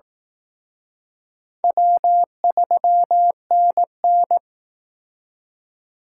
Morse code, heard as a keyed tone, about 18 wpm, 700 Hz. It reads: W3NN
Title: Speed Racing - USA 1x2 Call Signs 15wpm